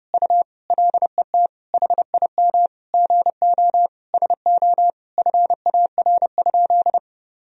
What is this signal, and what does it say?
Morse code, heard as a keyed tone, about 30 wpm, 700 Hz. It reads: U LET HIM GO SO FAR?
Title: Sentences from Top 300 Words and Abbreviations 30wpm